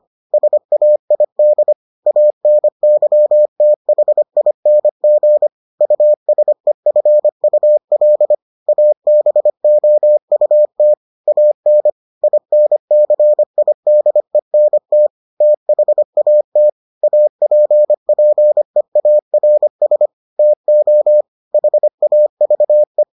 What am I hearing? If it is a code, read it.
Morse code, heard as a keyed tone, about 25 wpm, 600 Hz. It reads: SAID ANYTHING USEFUL ABOUT AN INCIDENT THAT APPEARS TO HAVE